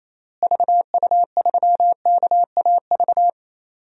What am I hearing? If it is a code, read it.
Morse code, heard as a keyed tone, about 28 wpm, 700 Hz. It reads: VU3KAV
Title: DXCC-100 Call Signs - 6 Characters - Repeated 3x - 28wpm